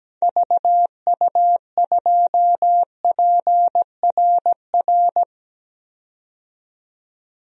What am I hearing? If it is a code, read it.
Morse code, heard as a keyed tone, about 17 wpm, 700 Hz. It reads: VU2PRR
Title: DXCC-100 Call Signs - 6 Characters - 17wpm